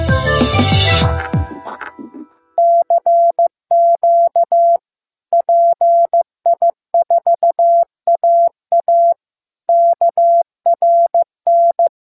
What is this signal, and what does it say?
Morse code, heard as a keyed tone, about 15 wpm, 680 Hz. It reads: CQ PI4AA KRN
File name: pi4aa / PI4AA Uitzending